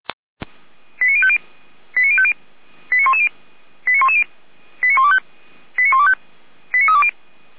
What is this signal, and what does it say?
Five-tone selective call (ZVEI), heard as quick runs of five tones, tones of about 70 ms. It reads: 89050 89050 89110 89110 89125 89125 89320
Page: SDR/ZVEI